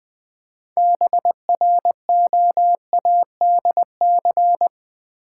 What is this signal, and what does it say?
Morse code, heard as a keyed tone, about 20 wpm, 700 Hz. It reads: BROADC
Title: Build A Word - 3 Letters At A Time - 9-Letter Words - 20wpm